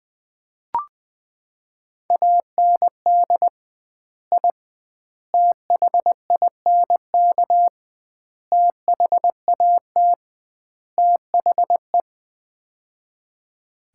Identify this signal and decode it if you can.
Morse code, heard as a keyed tone, about 20 wpm, 700 Hz. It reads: AND I THINK THAT THE
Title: 2x Word-Spacing - Sets of 5 Words - 20wpm